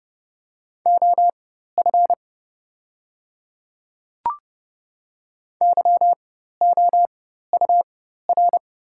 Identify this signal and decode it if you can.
Morse code, heard as a keyed tone, about 30 wpm, spaced as for 14 wpm, 700 Hz. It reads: OF YOUR
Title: 4x Character Spacing - Top 100 Words - 30wpm